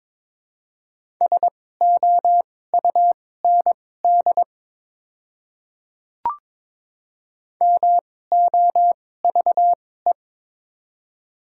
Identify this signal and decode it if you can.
Morse code, heard as a keyed tone, about 21 wpm, spaced as for 16 wpm, 700 Hz. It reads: SOUND MOVE
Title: 2x Character Spacing - Top 300 Words - 22wpm